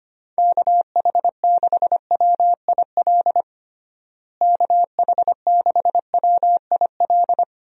Morse code, heard as a keyed tone, about 25 wpm, 700 Hz. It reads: KH6WIL KH6WIL
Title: USA Call Signs - 6 Characters - Repeated 3x - 25wpm